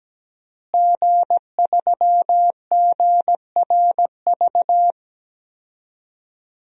Morse code, heard as a keyed tone, about 17 wpm, 700 Hz. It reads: G3GRV
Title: DXCC-100 Call Signs - 5 Characters - Repeated 3x - 17wpm